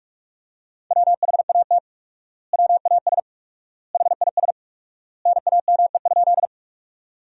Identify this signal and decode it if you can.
Morse code, heard as a keyed tone, about 42 wpm, 700 Hz. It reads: WHAT WAS HIS NAME?
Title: 4x Word-Spacing - Sentences from Top 200 Words - 45wpm